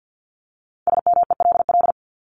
Morse code, heard as a keyed tone, about 50 wpm, 700 Hz. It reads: SMELL
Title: Top 1501-1750 Words 50wpm